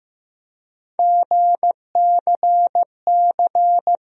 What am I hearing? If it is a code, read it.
Morse code, heard as a keyed tone, about 16 wpm, 700 Hz. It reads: GCC
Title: USA Call Sign Postfixes - 3 Letters 15wpm